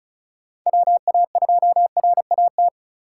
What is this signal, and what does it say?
Morse code, heard as a keyed tone, about 34 wpm, 700 Hz. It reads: WA2RAT
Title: USA Call Signs - 2x Character Spacing - 35wpm